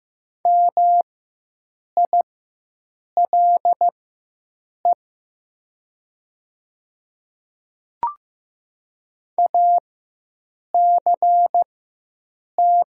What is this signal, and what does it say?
Morse code, heard as a keyed tone, about 15 wpm, spaced as for 7 wpm, 700 Hz. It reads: MILE ACT
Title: 4x Character Spacing - Top 300 Words - 15wpm